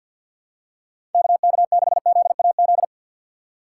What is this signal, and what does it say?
Morse code, heard as a keyed tone, about 52 wpm, 700 Hz. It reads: KK6ZAZ
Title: USA Call Signs - 6 Characters - 4x Character Spacing - 50wpm